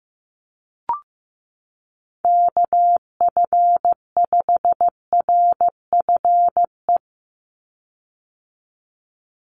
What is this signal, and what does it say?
Morse code, heard as a keyed tone, about 15 wpm, 700 Hz. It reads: KF5RFE
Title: USA Call Signs 15wpm